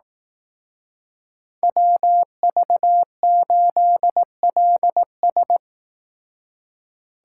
Morse code, heard as a keyed tone, about 18 wpm, 700 Hz. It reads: WV8LS
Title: Speed Racing - USA Call Signs 15wpm